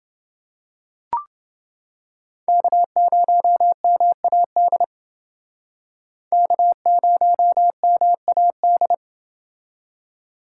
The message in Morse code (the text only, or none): K0MAD K0MAD